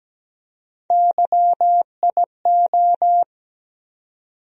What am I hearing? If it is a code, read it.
Morse code, heard as a keyed tone, about 17 wpm, 700 Hz. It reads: YIO
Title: USA Call Sign Postfixes - 3 Letters 17wpm